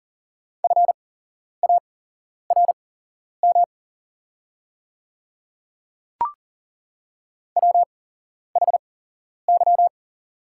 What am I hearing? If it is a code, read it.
Morse code, heard as a keyed tone, about 40 wpm, spaced as for 11 wpm, 700 Hz. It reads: FARM WHY
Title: 8x Character Spacing - Top 500 Words - 40wpm